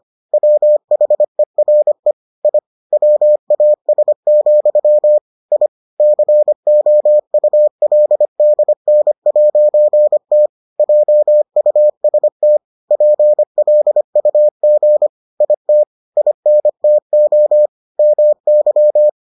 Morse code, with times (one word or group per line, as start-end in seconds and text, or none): 0.3-2.1 WHERE
2.5-2.6 I
2.9-5.2 WAS,
5.5-5.7 I
6.0-10.5 COULDN'T
10.8-12.6 JUST
12.9-15.1 PLUG
15.4-15.8 IT
16.2-17.7 INTO
18.0-19.1 MY